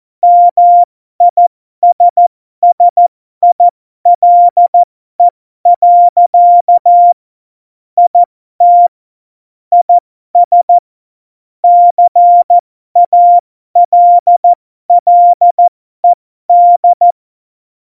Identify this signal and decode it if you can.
Morse code, heard as a keyed tone, about 14 wpm, spaced as for 12 wpm, 700 Hz. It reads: MISSILE. IT IS CALLED
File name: Morsekurs / Witze